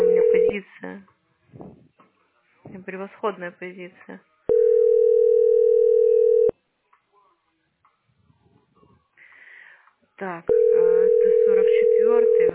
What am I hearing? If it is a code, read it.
Ringback tone (North America), 440 + 480 Hz, with speech.
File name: 00bf2699-0ecc-4892-8bdd